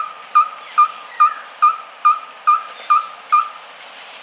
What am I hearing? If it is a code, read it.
There is no code.